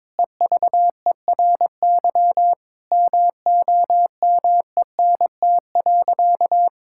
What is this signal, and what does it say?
Morse code, heard as a keyed tone, about 22 wpm, 700 Hz. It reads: EVERY MOMENT.
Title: Sentences from Top 1000 Words - Review Part 5 - 22wpm